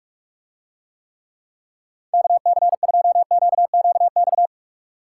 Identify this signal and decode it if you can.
Morse code, heard as a keyed tone, about 49 wpm, 700 Hz. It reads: KC2QQX